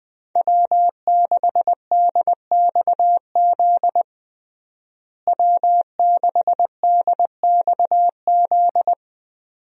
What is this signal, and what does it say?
Morse code, heard as a keyed tone, about 20 wpm, 700 Hz. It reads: W6DXZ W6DXZ
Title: USA Call Signs - 5 Characters - Repeated 2x - 20wpm